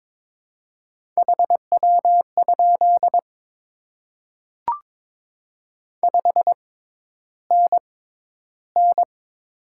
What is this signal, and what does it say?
Morse code, heard as a keyed tone, about 22 wpm, spaced as for 11 wpm, 700 Hz. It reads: HW? 5NN
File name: Top 20 QSO Elements - 6x Character Spacing - 22wpm